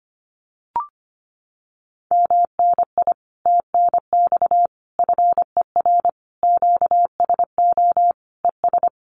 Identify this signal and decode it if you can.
Morse code, heard as a keyed tone, about 25 wpm, 700 Hz. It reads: MNI TNX FER QSO ES
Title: Common QSO Phrases Encore 25wpm